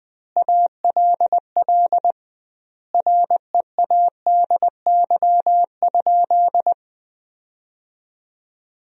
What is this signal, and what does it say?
Morse code, heard as a keyed tone, about 20 wpm, 700 Hz. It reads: ALL READY?
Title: 2x Word-Spacing - Sentences from Top 400 Words - 20wpm